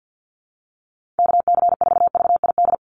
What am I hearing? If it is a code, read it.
Morse code, heard as a keyed tone, about 50 wpm, 700 Hz. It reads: KC4VID
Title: USA Call Signs 50wpm